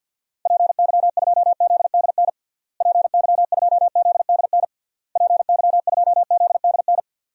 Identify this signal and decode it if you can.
Morse code, heard as a keyed tone, about 50 wpm, 700 Hz. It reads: PY2ZDN PY2ZDN PY2ZDN
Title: DXCC-100 Call Signs - 6 Characters - Repeated 3x - 50wpm